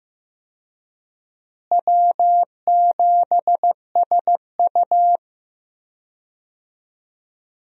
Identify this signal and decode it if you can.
Morse code, heard as a keyed tone, about 15 wpm, 700 Hz. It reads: W7SU
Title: USA Call Signs - 4 Characters - Repeated 3x - 15wpm